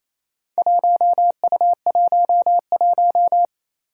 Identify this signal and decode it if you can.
Morse code, heard as a keyed tone, about 28 wpm, 700 Hz. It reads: 1U11
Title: Four Challenging Letter-Number Combinations 28wpm